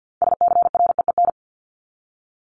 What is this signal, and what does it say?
Morse code, heard as a keyed tone, about 50 wpm, 700 Hz. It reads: SCREEN
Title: Top 751-1000 Words 50wpm